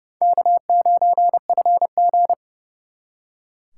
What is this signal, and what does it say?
Morse code, heard as a keyed tone, about 40 wpm, spaced as for 35 wpm, 700 Hz. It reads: K9FG
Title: Speed Racing - USA 1x2 Call Signs 30wpm